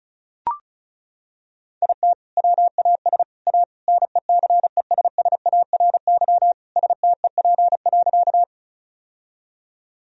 Morse code, heard as a keyed tone, about 36 wpm, 700 Hz. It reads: IT WAS A NECESSARY STEP.
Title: Sentences from Top 1000 Words - Review Part 2 - 35wpm